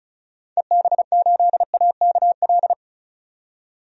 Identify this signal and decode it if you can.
Morse code, heard as a keyed tone, about 36 wpm, 700 Hz. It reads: EB8AKL
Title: DXCC-100 Call Signs - 6 Characters - 35wpm